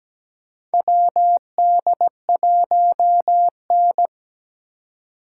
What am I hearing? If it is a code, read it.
Morse code, heard as a keyed tone, about 17 wpm, 700 Hz. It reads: WD1N
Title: USA Call Signs - 4 Characters - 3x Character Spacing - 17wpm